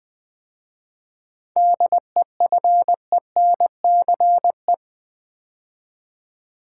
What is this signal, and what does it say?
Morse code, heard as a keyed tone, about 20 wpm, 690 Hz. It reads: DEFENCE
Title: 7 Letter Words - First 3 Letters Given - 20wpm